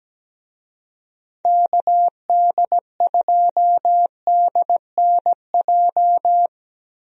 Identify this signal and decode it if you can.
Morse code, heard as a keyed tone, about 17 wpm, 700 Hz. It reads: KD2DNJ
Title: USA Call Signs - 6 Characters - Repeated 2x - 17wpm